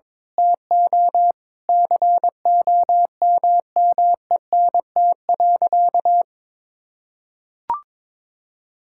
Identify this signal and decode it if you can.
Morse code, heard as a keyed tone, about 22 wpm, 700 Hz. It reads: TO COMMENT.
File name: practice / 22wpm / Sentences with 8-Word Phrases - 22wpm